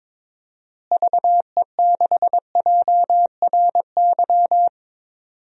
Morse code, heard as a keyed tone, about 22 wpm, 700 Hz. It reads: VE6JRY
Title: DXCC-100 Call Signs - 6 Characters - Repeated 3x - 22wpm